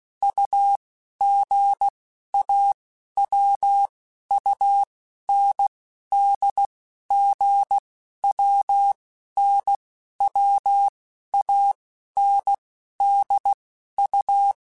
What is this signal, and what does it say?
Morse code, heard as a keyed tone, about 16 wpm, 800 Hz. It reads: U G A W U N D G W N W A N D U
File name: Basic Letter Groups By7Groups / AWUNDG / AWUNDG@15